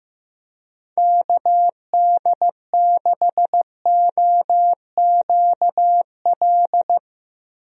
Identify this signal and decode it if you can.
Morse code, heard as a keyed tone, about 15 wpm, 700 Hz. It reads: KD6OQL
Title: USA Call Signs - 6 Characters - 15wpm